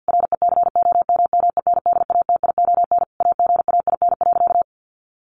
Morse code, heard as a keyed tone, about 50 wpm, 700 Hz. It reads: RECOMMENDATION AGAIN.
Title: Sentences from Top 1400 Words 50wpm